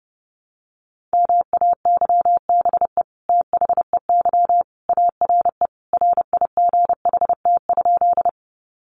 Morse code, heard as a keyed tone, about 30 wpm, 700 Hz. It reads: MAYBE THEY ARE RIGHT?
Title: Sentences from Top 1000 Words 30wpm